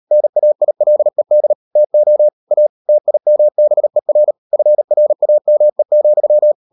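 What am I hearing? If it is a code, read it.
Morse code, heard as a keyed tone, about 38 wpm, 600 Hz. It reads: NAILED TO A TIMBER FRAME,
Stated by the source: Morse code practice at 38 WPM, sending Ken Follett's novel Never in English, from CW med Gnister.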